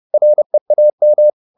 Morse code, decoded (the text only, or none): REAM